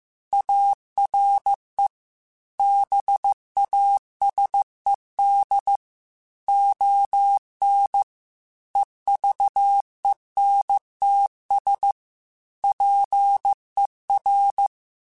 Morse code, heard as a keyed tone, about 15 wpm, 800 Hz. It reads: ARE BASED ON EVENTS PER
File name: Technical Subjects / Technical Subjects-mp3s / Reactance @13